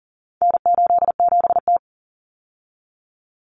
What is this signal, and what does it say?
Morse code, heard as a keyed tone, about 40 wpm, 700 Hz. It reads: N87T
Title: Common Cut Numbers - 4-Digit 40wpm